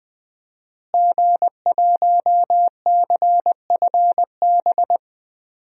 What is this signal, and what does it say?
Morse code, heard as a keyed tone, about 21 wpm, 700 Hz. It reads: G1CFB
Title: DXCC-100 Call Signs - 5 Characters - Repeated 2x - 20wpm